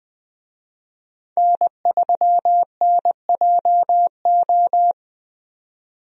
Morse code, heard as a keyed tone, about 20 wpm, 700 Hz. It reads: N3NJO
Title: USA Call Signs - 5 Characters - 20wpm